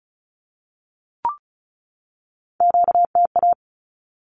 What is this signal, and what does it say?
Morse code, heard as a keyed tone, about 34 wpm, 700 Hz. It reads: QTA
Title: Q-Codes 35wpm